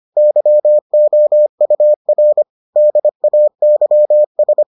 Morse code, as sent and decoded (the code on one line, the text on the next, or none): -.-- --- ..- .-. / -.. .- -.-- ...
YOUR DAYS